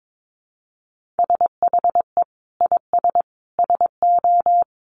Morse code, heard as a keyed tone, about 22 wpm, 700 Hz. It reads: SHE IS SO
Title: Sentences from Top 1500 Words 22wpm